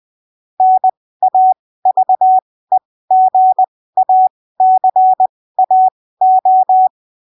Morse code, heard as a keyed tone, about 20 wpm, spaced as for 15 wpm, 750 Hz. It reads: NAVEGACAO
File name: arquivos / texto / texto 03 20ppm